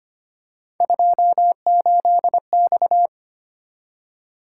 Morse code, heard as a keyed tone, about 25 wpm, 700 Hz. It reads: 28X
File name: Three Letter-Number Combinations - 25wpm